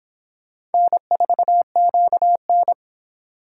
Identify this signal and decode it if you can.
Morse code, heard as a keyed tone, about 26 wpm, 700 Hz. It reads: N4QN